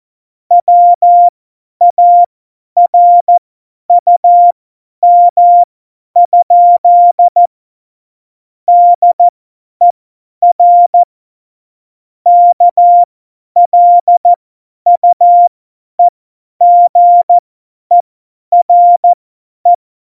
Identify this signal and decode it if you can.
Morse code, heard as a keyed tone, about 14 wpm, spaced as for 10 wpm, 700 Hz. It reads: WARUM? DER KLUEGERE
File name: Morsekurs / Witze / Witz 04 Juli 17, Tempo50 F70